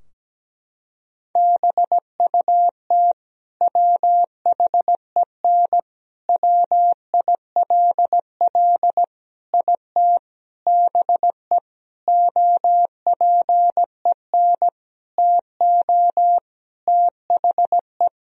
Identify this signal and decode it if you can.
Morse code, heard as a keyed tone, about 17 wpm, 700 Hz. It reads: BUT WHEN WILL IT BE OPEN TO THE
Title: Sentences from Top 1000 Words - Review Part 3 - 17wpm